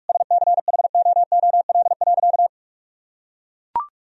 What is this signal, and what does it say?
Morse code, heard as a keyed tone, about 41 wpm, 700 Hz. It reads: SCHOOL.
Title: Sentences from Top 1000 Words - Review Part 1 - 45wpm